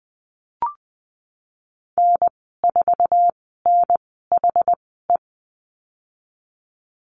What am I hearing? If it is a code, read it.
Morse code, heard as a keyed tone, about 20 wpm, spaced as for 15 wpm, 700 Hz. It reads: N4NHE
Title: 069 Lesson 12g - Letter d - Call Signs 20wpm